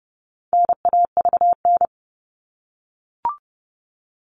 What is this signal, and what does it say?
Morse code, heard as a keyed tone, about 30 wpm, 700 Hz. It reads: NAVN
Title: All Cut Numbers - 4-Digit 30wpm